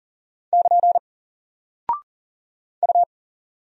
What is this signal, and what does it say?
Morse code, heard as a keyed tone, about 40 wpm, 700 Hz. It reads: <KN> U
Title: Single Character (Letter:Number:Punctuation:Prosign) Alt 40wpm